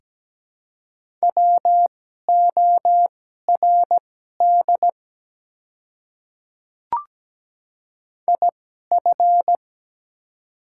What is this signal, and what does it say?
Morse code, heard as a keyed tone, about 17 wpm, spaced as for 12 wpm, 700 Hz. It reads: WORD IF
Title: 2x Character Spacing - Top 100 Words - 17wpm